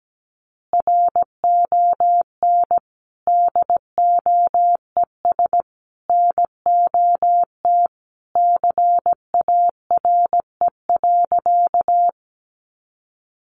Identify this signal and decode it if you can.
Morse code, heard as a keyed tone, about 17 wpm, 700 Hz. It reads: RON DOES NOT CARE.
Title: Additional Easy Phrases 17wpm